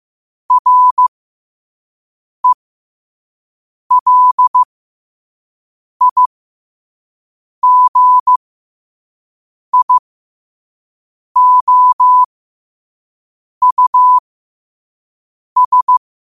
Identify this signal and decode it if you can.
Morse code, heard as a keyed tone, about 15 wpm, spaced as for 5 wpm, 1000 Hz. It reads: RELIGIOUS
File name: qotdfiles / 5wpm